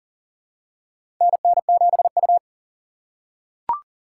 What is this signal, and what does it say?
Morse code, heard as a keyed tone, about 37 wpm, 700 Hz. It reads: NN7U
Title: USA Call Signs - 4 Characters - 4x Character Spacing - 40wpm